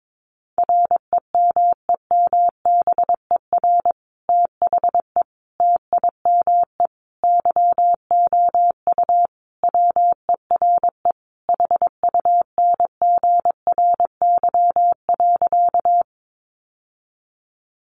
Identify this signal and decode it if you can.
Morse code, heard as a keyed tone, about 22 wpm, 700 Hz. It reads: REMEMBER THE TIME YOU WERE HUNGRY.
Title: Sentences from Top 1400 Words 22wpm